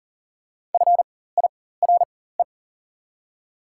Morse code, heard as a keyed tone, about 40 wpm, spaced as for 19 wpm, 700 Hz. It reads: FIRE
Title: 4x Character Spacing - Top 500 Words - 40wpm